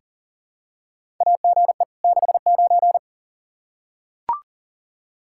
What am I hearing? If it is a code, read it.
Morse code, heard as a keyed tone, about 40 wpm, 700 Hz. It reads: AGE 69
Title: Practice QSO - Exchange 3 - More Info - 40wpm